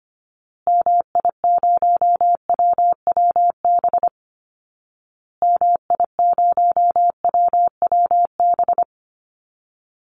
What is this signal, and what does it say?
Morse code, heard as a keyed tone, about 25 wpm, 700 Hz. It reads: MI0WWB MI0WWB
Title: Call Signs Repeated 2x - 25wpm